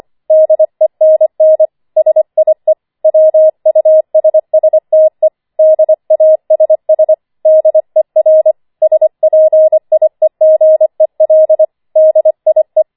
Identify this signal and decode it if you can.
Morse code, heard as a keyed tone, about 25 wpm, 630 Hz. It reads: DENN SIE WUSSTE DASS DER SPIEGEL DIE
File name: 25wpm